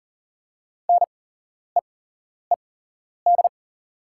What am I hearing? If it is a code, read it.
Morse code, heard as a keyed tone, about 40 wpm, spaced as for 11 wpm, 690 Hz. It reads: NEED
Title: 8x Character Spacing - Top 300 Words - 40wpm